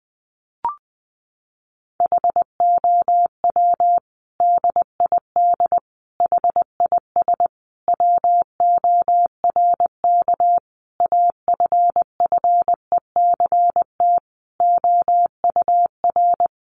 Morse code, heard as a keyed tone, about 20 wpm, 700 Hz. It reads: HOW DID HIS WORK AFFECT OUR
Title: Sentences from Top 900 Words 20wpm